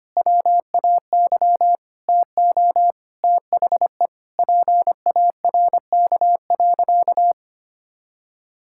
Morse code, heard as a keyed tone, about 25 wpm, 700 Hz. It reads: WAY TO THE PARK.
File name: practice / 25wpm / Sentences with 4-Word Phrases - 25wpm